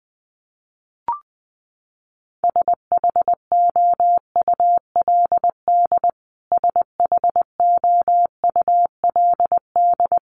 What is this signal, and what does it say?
Morse code, heard as a keyed tone, about 20 wpm, 700 Hz. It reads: SHOULD SHOULD
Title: Top 500 Words Repeated 20wpm